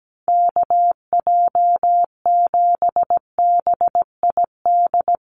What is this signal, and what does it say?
Morse code, heard as a keyed tone, about 17 wpm, 700 Hz. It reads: KJ7BID
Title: USA Call Signs Repeated 3x 17wpm